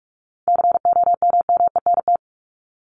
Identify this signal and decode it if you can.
Morse code, heard as a keyed tone, about 44 wpm, 700 Hz. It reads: COMMENT
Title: Top 1751-2000 Words 45wpm